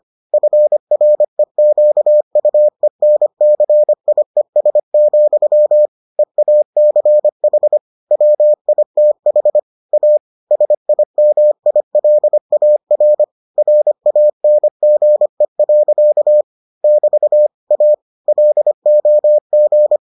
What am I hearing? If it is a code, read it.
Morse code, heard as a keyed tone, about 25 wpm, 600 Hz. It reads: FREQUENCIES, EACH WITH A SIMILAR RANGE. <BT> A LOG